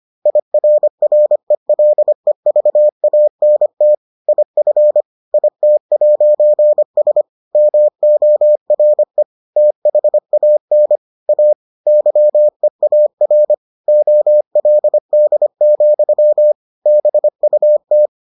Morse code, heard as a keyed tone, about 25 wpm, 600 Hz. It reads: IRRELEVANT IF IT'S MORE THAN A YEAR OLD, BUT